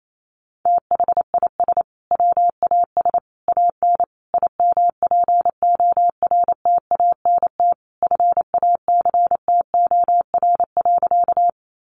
Morse code, heard as a keyed tone, about 28 wpm, 700 Hz. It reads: THIS WAS AN IMPORTANT FACTOR.
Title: Sentences from Top 1300 Words 28wpm